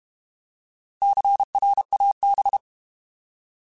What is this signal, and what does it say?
Morse code, heard as a keyed tone, about 32 wpm, 800 Hz. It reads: CRAB
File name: Alphabetical / Alphabetical mp3s / Cs @30